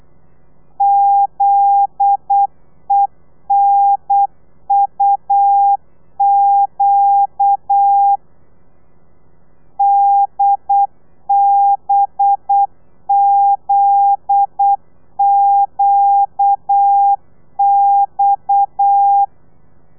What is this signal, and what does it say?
Morse code, heard as a keyed tone, about 8 wpm, 800 Hz. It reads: ZENUQ DBZQX
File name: telegrafia / i2tfj / Lez9 35